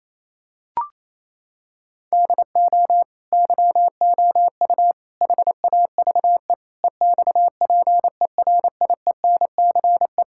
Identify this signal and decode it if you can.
Morse code, heard as a keyed tone, about 28 wpm, 700 Hz. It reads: DO YOU HAVE EXPERIENCE
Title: Sentences from Top 1000 Words - Review Part 5 - 28wpm